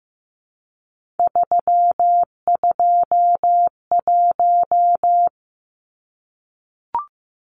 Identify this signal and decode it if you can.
Morse code, heard as a keyed tone, about 15 wpm, 700 Hz. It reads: 321
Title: Numbers - 3-Digit 15wpm